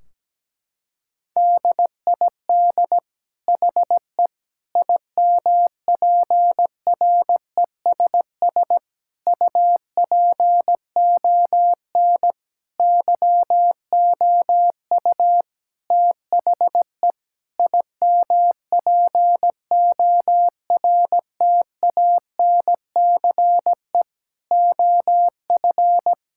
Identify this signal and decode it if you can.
Morse code, heard as a keyed tone, about 17 wpm, 700 Hz. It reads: DID HE IMPRESS UPON YOU THE IMPORTANCE OF
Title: Sentences from Top 1600 Words 17wpm